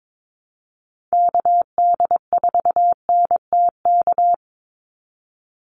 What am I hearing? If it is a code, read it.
Morse code, heard as a keyed tone, about 22 wpm, 700 Hz. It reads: KD4NTK
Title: USA Call Signs 22wpm